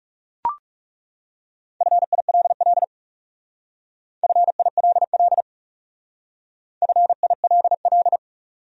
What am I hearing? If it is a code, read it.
Morse code, heard as a keyed tone, about 38 wpm, 700 Hz. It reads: FILL FILL FILL